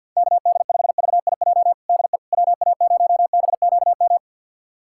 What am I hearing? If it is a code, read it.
Morse code, heard as a keyed tone, about 52 wpm, 690 Hz. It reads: KD5VIJ DE WA0BQM
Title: USA Callsign DE Callsign - 6 Characters - 50wpm